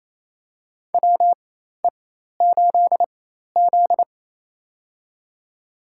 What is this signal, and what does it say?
Morse code, heard as a keyed tone, about 28 wpm, spaced as for 13 wpm, 700 Hz. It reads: WE8Z
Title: USA Call Signs - 4 Characters - 4x Character Spacing - 28wpm